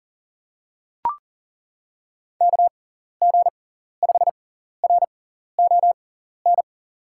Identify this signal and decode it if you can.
Morse code, heard as a keyed tone, about 40 wpm, spaced as for 14 wpm, 700 Hz. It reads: KG5RON
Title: USA Call Signs - 6x Character Spacing - 40wpm